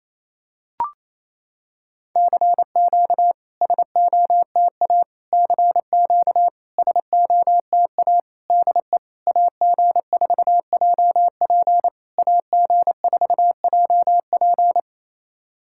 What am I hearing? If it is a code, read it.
Morse code, heard as a keyed tone, about 28 wpm, 700 Hz. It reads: CQ SOTA CQ SOTA DE AG4JP AG4JP
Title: SOTA QSO Practice 28wpm